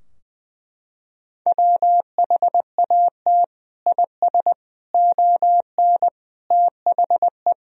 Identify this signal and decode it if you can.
Morse code, heard as a keyed tone, about 20 wpm, 700 Hz. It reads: WHAT IS ON THE
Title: Sentences from Top 1000 Words - Review Part 2 - 20wpm